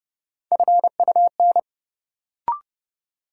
Morse code, heard as a keyed tone, about 29 wpm, 700 Hz. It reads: FUN
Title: Sentences from Top 700 Words - New Word Drill - 30wpm